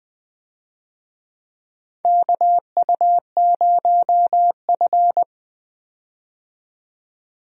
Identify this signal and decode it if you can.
Morse code, heard as a keyed tone, about 20 wpm, 700 Hz. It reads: KU0F
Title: USA Call Signs - 4 Characters - 20wpm